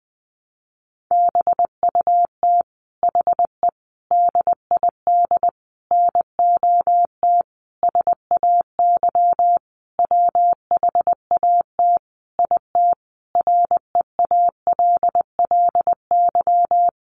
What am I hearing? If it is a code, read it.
Morse code, heard as a keyed tone, about 20 wpm, 700 Hz. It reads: BUT HE DID NOT SAY WHAT IT REALLY
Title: Sentences from Top 600 Words 20wpm